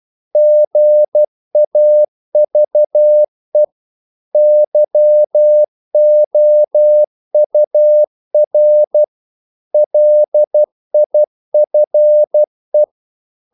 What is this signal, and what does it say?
Morse code, heard as a keyed tone, about 12 wpm, 600 Hz. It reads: GAVE YOUR LIFE
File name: CW / Practice / If12